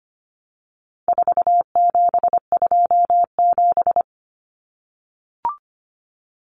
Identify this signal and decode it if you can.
Morse code, heard as a keyed tone, about 25 wpm, 700 Hz. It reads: 4727
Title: Numbers - 4-Digit 25wpm